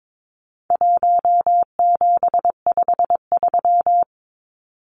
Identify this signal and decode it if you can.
Morse code, heard as a keyed tone, about 22 wpm, 700 Hz. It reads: 1753
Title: Numbers - 4-Digit 22wpm